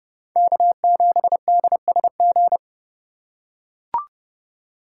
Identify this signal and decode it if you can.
Morse code, heard as a keyed tone, about 30 wpm, 700 Hz. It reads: K7DSG
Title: USA Call Signs - 5 Characters - 4x Character Spacing - 30wpm